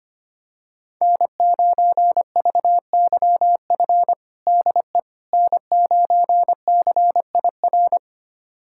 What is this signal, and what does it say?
Morse code, heard as a keyed tone, about 25 wpm, 700 Hz. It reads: N9VYF DE N9CIR
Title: USA Callsign DE Callsign - 5 Characters - 25wpm